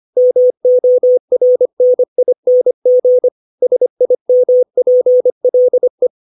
Morse code, heard as a keyed tone, about 25 wpm, 500 Hz. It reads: MORNING SIMPLE
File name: 632 words / 632 Words 25WPM